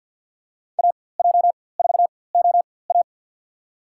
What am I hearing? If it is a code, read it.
Morse code, heard as a keyed tone, about 52 wpm, spaced as for 23 wpm, 700 Hz. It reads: AJ4OA
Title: USA Call Signs - 4x Character Spacing - 50wpm